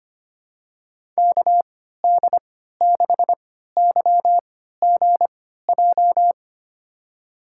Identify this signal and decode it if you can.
Morse code, heard as a keyed tone, about 25 wpm, spaced as for 14 wpm, 700 Hz. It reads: KD6YGJ
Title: USA Call Signs - 6 Characters - 3x Character Spacing - 25wpm